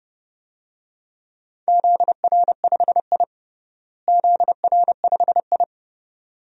Morse code, heard as a keyed tone, about 32 wpm, 700 Hz. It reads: ZR5I ZR5I
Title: DXCC-100 Call Signs - 4 Characters - Repeated 2x - 30wpm